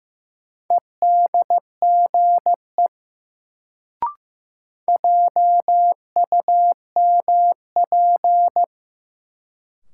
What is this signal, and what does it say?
Morse code, heard as a keyed tone, about 15 wpm, 700 Hz. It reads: EDGE JUMP
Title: Sentences from Top 1300 Words - New Word Drill - 15wpm